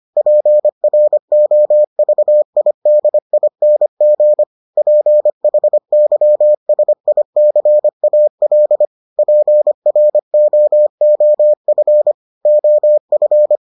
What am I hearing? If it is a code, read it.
Morse code, heard as a keyed tone, about 25 wpm, 600 Hz. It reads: PROVIDING PHYSICAL PROOF OF